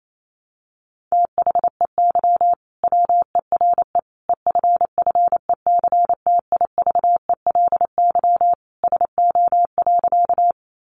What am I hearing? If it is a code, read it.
Morse code, heard as a keyed tone, about 28 wpm, 700 Hz. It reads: THEY WERE EFFECTIVELY SO.
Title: Sentences from Top 1400 Words 28wpm